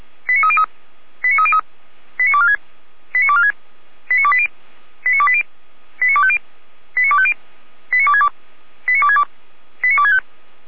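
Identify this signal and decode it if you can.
Five-tone selective call (ZVEI), heard as quick runs of five tones, tones of about 70 ms. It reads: 89303 89303 89246 89246 89280 89280 89250 89250 89262 89262 89265
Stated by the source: ZVEI five-tone call 89220, received by SDR, from the feuerwehr-orlamuende.org fire brigade audio server.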